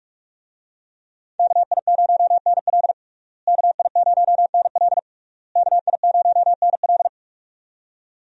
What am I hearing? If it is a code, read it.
Morse code, heard as a keyed tone, about 46 wpm, 700 Hz. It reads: KI0NL KI0NL KI0NL
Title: USA Call Signs - 5 Characters - Repeated 3x - 45wpm